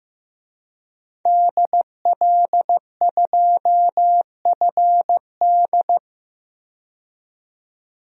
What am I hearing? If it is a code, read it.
Morse code, heard as a keyed tone, about 15 wpm, 700 Hz. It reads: DL2FD
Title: DXCC-100 Call Signs - 5 Characters - 15wpm